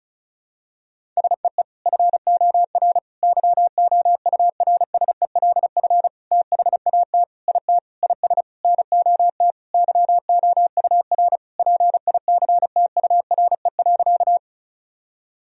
Morse code, heard as a keyed tone, about 36 wpm, 700 Hz. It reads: SEE FOR YOURSELF THAT IT IS NOT YOUR PICTURE.
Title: Sentences from Top 1000 Words - Review Part 1 - 35wpm